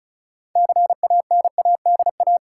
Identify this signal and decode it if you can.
Morse code, heard as a keyed tone, about 36 wpm, 700 Hz. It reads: CANADA
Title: Sets of 5 Words - New Word Drill - 35wpm